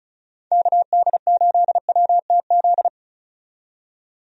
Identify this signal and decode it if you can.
Morse code, heard as a keyed tone, about 34 wpm, 700 Hz. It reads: KD8WTZ
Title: USA Call Signs - 6 Characters - 35wpm